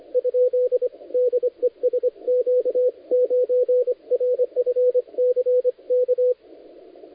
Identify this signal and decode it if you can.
Morse code, heard as a keyed tone, about 23 wpm, 500 Hz, with radio noise. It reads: ? DE SQ9RFC K
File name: dlazaawansowanych / nagraniazpasma / titawaNETcontrol